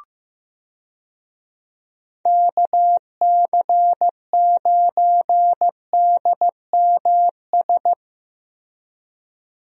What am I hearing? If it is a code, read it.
Morse code, heard as a keyed tone, about 15 wpm, 700 Hz. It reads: KC9DMS
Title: USA Call Signs - 6 Characters - Repeated 3x - 15wpm